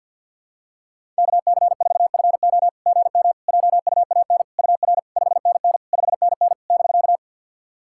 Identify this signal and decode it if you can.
Morse code, heard as a keyed tone, about 52 wpm, 690 Hz. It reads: KC4FO GM JUAN UR 5NN 5NN <BK>
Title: SOTA QSO Practice 50wpm